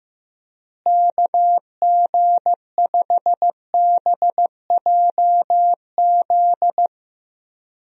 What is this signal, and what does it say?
Morse code, heard as a keyed tone, about 15 wpm, 700 Hz. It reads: KG5BJZ
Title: USA Call Signs - 6 Characters - Repeated 2x - 15wpm